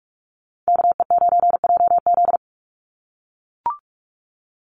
Morse code, heard as a keyed tone, about 44 wpm, 700 Hz. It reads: KE9JZ
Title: USA Call Signs 45wpm